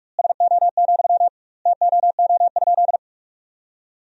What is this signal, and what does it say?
Morse code, heard as a keyed tone, about 46 wpm, 700 Hz. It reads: SO, TOO?
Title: 2x Word-Spacing - Sentences from Top 200 Words - 45wpm